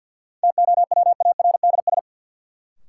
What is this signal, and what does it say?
Morse code, heard as a keyed tone, about 53 wpm, 700 Hz. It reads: TOWARDS
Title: Sentences from Top 1000 Words - Review Part 2 - New Word Drill - 50wpm